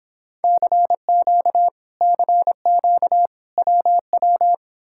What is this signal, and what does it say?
Morse code, heard as a keyed tone, about 30 wpm, 700 Hz. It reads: CQ CQ WW